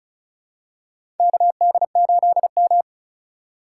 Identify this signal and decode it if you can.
Morse code, heard as a keyed tone, about 36 wpm, 700 Hz. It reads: KD8M
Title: USA Call Signs - 4 Characters - 4x Character Spacing - 35wpm